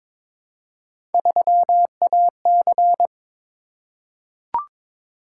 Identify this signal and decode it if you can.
Morse code, heard as a keyed tone, about 22 wpm, 700 Hz. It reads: 3AC
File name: practice / 22wpm / Three Letter-Number Combinations - 22wpm